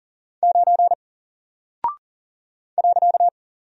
Morse code, heard as a keyed tone, about 40 wpm, 700 Hz. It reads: 9 .
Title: Single Character (Letter:Number:Punctuation:Prosign) 40wpm